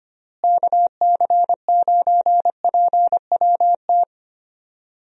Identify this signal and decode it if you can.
Morse code, heard as a keyed tone, about 25 wpm, 700 Hz. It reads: KC9PWT
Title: USA Call Signs - 6 Characters - Repeated 2x - 25wpm